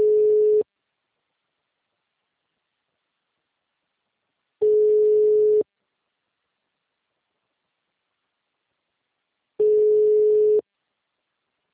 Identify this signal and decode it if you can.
Ringback tone (Europe, 425 Hz).
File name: out- -2030- - - .8187